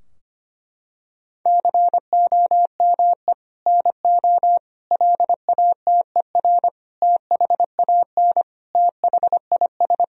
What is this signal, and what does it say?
Morse code, heard as a keyed tone, about 25 wpm, 700 Hz. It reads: COME NO LATER THAN THIS
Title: Sentences from Top 1000 Words - Review Part 4 - 25wpm